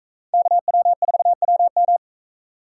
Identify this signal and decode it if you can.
Morse code, heard as a keyed tone, about 41 wpm, 700 Hz. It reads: KW4WM